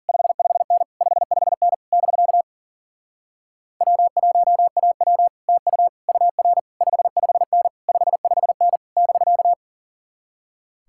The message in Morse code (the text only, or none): 55N 55N <BK> W1AW TU UR 55N 55N <BK>